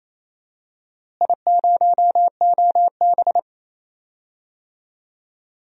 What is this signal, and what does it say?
Morse code, heard as a keyed tone, about 28 wpm, 700 Hz. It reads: I0OB
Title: DXCC-100 Call Signs - 4 Characters - Repeated 3x - 28wpm